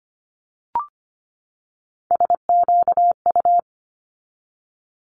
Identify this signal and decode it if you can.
Morse code, heard as a keyed tone, about 25 wpm, 700 Hz. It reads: SQU
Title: Multi-letter Phonemes 25wpm